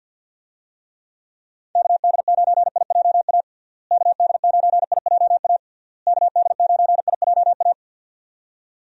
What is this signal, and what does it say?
Morse code, heard as a keyed tone, about 52 wpm, 700 Hz. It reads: KD9IJA KD9IJA KD9IJA
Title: USA Call Signs - 6 Characters - Repeated 3x - 50wpm